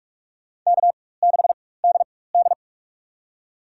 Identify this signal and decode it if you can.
Morse code, heard as a keyed tone, about 41 wpm, spaced as for 20 wpm, 700 Hz. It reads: K6DD
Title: USA Call Signs - 4 Characters - 4x Character Spacing - 45wpm